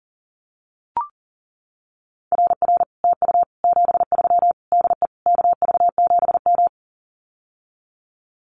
Morse code, heard as a keyed tone, about 40 wpm, 700 Hz. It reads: RR TU 73 DE KV7M
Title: Common QSO Phrases Encore 40wpm